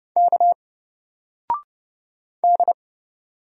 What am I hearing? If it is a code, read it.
Morse code, heard as a keyed tone, about 30 wpm, 700 Hz. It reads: K D
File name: Single-Letter-Number-30wpm